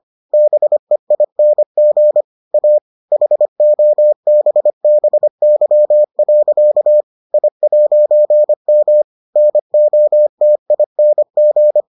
Morse code, heard as a keyed tone, about 25 wpm, 600 Hz. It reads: BEING A HOBBY. I'M NOTING